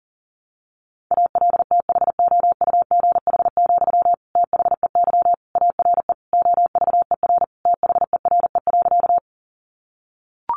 Morse code, heard as a keyed tone, about 40 wpm, 700 Hz. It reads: ALTHOUGH, THEY ARE OVER THERE.
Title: Sentences from Top 600 Words 40wpm